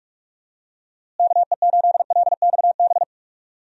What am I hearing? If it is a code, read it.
Morse code, heard as a keyed tone, about 46 wpm, 700 Hz. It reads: KE8LXB
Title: USA Call Signs - 6 Characters - Repeated 2x - 45wpm